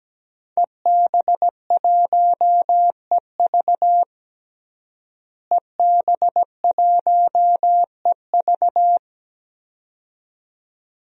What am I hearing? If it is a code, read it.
Morse code, heard as a keyed tone, about 17 wpm, 700 Hz. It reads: EB1EV EB1EV
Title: DXCC-100 Call Signs - 5 Characters - Repeated 2x - 17wpm